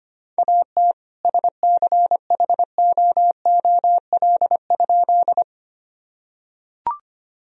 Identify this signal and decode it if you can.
Morse code, heard as a keyed tone, about 25 wpm, 700 Hz. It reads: AT SCHOOL?
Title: Sentences from Top 1000 Words - Review Part 1 - 25wpm